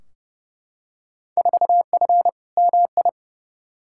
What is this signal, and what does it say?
Morse code, heard as a keyed tone, about 30 wpm, 700 Hz. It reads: 4F MI
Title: Field Day Exchanges - 30wpm